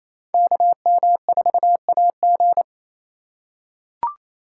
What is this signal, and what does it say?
Morse code, heard as a keyed tone, about 28 wpm, 700 Hz. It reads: KM4AG
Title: USA Call Signs - 4x Character Spacing - 28wpm